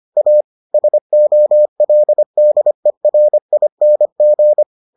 Morse code, heard as a keyed tone, about 25 wpm, 600 Hz. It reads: A SOLDERING